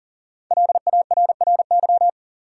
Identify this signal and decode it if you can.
Morse code, heard as a keyed tone, about 40 wpm, 700 Hz. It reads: LARRY
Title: USA Common Names - Max 5 Letters - 40wpm